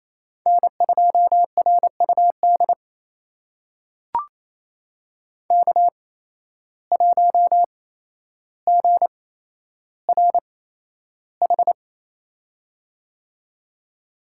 Morse code, heard as a keyed tone, about 28 wpm, spaced as for 12 wpm, 700 Hz. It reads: N2RUD K1GRH
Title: USA Call Signs - 8x Character Spacing - 28wpm